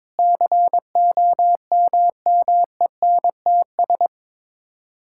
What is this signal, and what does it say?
Morse code, heard as a keyed tone, about 22 wpm, 700 Hz. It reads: COMMENTS
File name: Sets of 9 Words - 22wpm